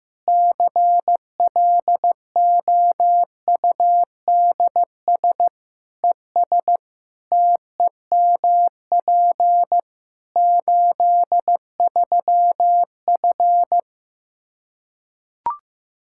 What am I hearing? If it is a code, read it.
Morse code, heard as a keyed tone, about 15 wpm, 700 Hz. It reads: CLOUDS ES TEMP 83F
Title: Practice QSO - Exchange 2 - Info - 15wpm